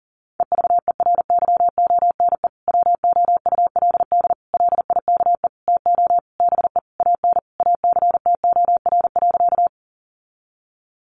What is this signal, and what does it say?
Morse code, heard as a keyed tone, about 40 wpm, 700 Hz. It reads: EVERYONE WOULD LIKE TO BE AN ACTOR.
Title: Sentences from Top 1500 Words 40wpm